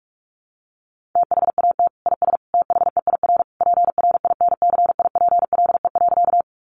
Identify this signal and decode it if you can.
Morse code, heard as a keyed tone, about 44 wpm, 700 Hz. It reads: THAT IS THEIR PRINCIPLE.
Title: Sentences from Top 1400 Words 45wpm